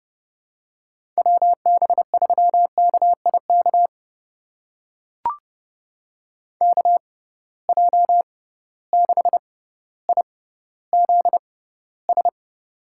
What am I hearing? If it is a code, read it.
Morse code, heard as a keyed tone, about 31 wpm, spaced as for 15 wpm, 700 Hz. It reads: WB3KIK KJ6IZS
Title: USA Call Signs - 6 Characters - 6x Character Spacing - 30wpm